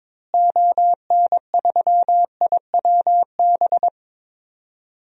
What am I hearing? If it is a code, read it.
Morse code, heard as a keyed tone, about 22 wpm, 700 Hz. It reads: ON3IWB
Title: DXCC-100 Call Signs - 6 Characters - Repeated 3x - 22wpm